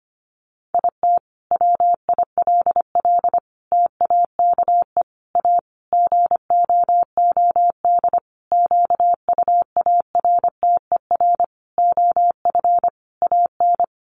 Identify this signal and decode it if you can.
Morse code, heard as a keyed tone, about 25 wpm, 700 Hz. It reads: IT WILL TAKE A GOOD QUARTER OF AN
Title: Sentences from Top 1300 Words 25wpm